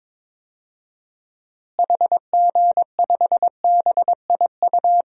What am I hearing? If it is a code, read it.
Morse code, heard as a keyed tone, about 22 wpm, 700 Hz. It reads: HG5BIU
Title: DXCC-100 Call Signs - 6 Characters - Repeated 3x - 22wpm